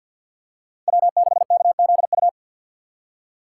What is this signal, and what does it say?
Morse code, heard as a keyed tone, about 49 wpm, 690 Hz. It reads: W6KZU
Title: USA Call Signs - 6x Character Spacing - 50wpm